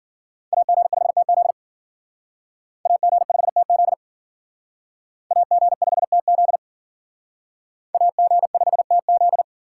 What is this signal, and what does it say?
Morse code, heard as a keyed tone, about 51 wpm, 700 Hz. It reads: AG5TZ AG5TZ AG5TZ AG5TZ